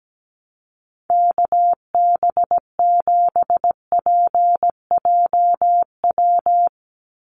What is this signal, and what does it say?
Morse code, heard as a keyed tone, about 17 wpm, 700 Hz. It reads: KB7PJW
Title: USA Call Signs 17wpm